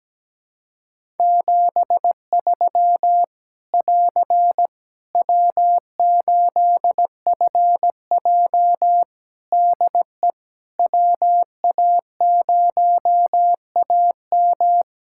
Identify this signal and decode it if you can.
Morse code, heard as a keyed tone, about 17 wpm, 700 Hz. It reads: 73 <AR> W8FJ DE WA0AM
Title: Practice QSO - Exchange 4 - Ending - 17wpm